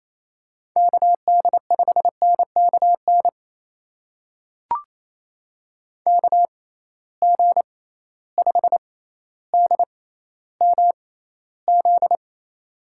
Morse code, heard as a keyed tone, about 28 wpm, spaced as for 14 wpm, 700 Hz. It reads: KD5NKN KG5DMZ
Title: USA Call Signs - 6 Characters - 6x Character Spacing - 28wpm